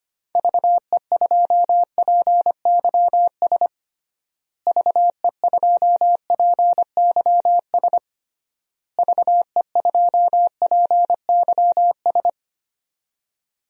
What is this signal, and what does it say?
Morse code, heard as a keyed tone, about 25 wpm, 700 Hz. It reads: VE2PYS VE2PYS VE2PYS
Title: DXCC-100 Call Signs - 6 Characters - Repeated 3x - 25wpm